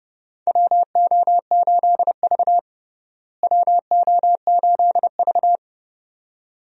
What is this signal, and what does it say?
Morse code, heard as a keyed tone, about 30 wpm, 700 Hz. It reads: WO8V WO8V
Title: USA Call Signs - 4 Characters - Repeated 2x - 30wpm